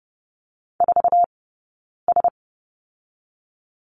Morse code, heard as a keyed tone, about 30 wpm, spaced as for 9 wpm, 700 Hz. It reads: 4S
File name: Lesson 20d - Number 3 - Two Characters 30wpm